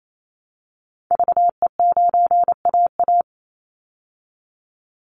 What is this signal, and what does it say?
Morse code, heard as a keyed tone, about 28 wpm, 700 Hz. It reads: VE9AA
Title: Call Signs Repeated 3x - 28wpm